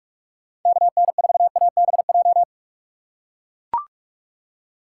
Morse code, heard as a keyed tone, about 41 wpm, 700 Hz. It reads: KN4ABJ
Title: USA Call Signs - 6 Characters - 3x Character Spacing - 45wpm